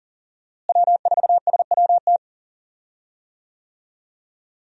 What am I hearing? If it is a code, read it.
Morse code, heard as a keyed tone, about 37 wpm, 700 Hz. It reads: W4SWT
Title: USA Call Signs - 5 Characters - Repeated 2x - 40wpm